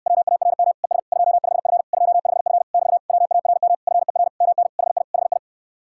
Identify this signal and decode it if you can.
Morse code, heard as a keyed tone, about 69 wpm, 700 Hz. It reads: WARM ES 25F 25F <BT> WARM FER MN HI HI